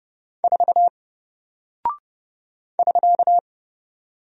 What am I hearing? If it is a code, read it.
Morse code, heard as a keyed tone, about 29 wpm, 700 Hz. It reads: 4 <SK>
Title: Single Character (Letter:Number:Punctuation:Prosign) 30wpm